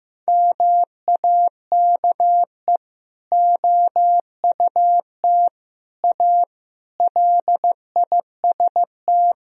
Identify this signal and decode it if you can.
Morse code, heard as a keyed tone, about 15 wpm, 700 Hz. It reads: MAKE OUT A LIST
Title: Sentences from Top 1000 Words - Review Part 2 - 15wpm